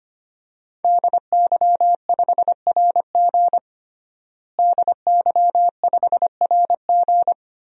Morse code, heard as a keyed tone, about 25 wpm, 700 Hz. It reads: DY5RG DY5RG
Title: DXCC-100 Call Signs - 5 Characters - Repeated 2x - 25wpm